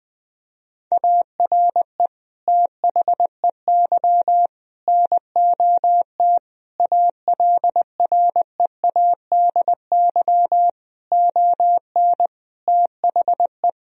Morse code, heard as a keyed tone, about 20 wpm, 700 Hz. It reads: ARE THEY NOT ALREADY ON THE
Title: Sentences from Top 1000 Words - Review Part 1 - 20wpm